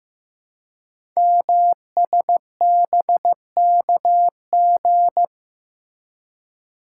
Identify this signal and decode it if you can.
Morse code, heard as a keyed tone, about 15 wpm, 700 Hz. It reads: MSBKG
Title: Code Groups - 5 Letters - 15wpm